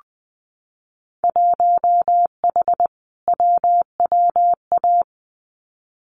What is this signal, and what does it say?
Morse code, heard as a keyed tone, about 20 wpm, 700 Hz. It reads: 1H WWA
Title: Winter Field Day Exchanges 20wpm